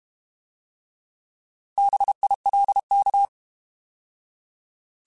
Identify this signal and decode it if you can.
Morse code, heard as a keyed tone, about 32 wpm, 800 Hz. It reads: BILK
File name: Alphabetical / Alphabetical mp3s / Bs @30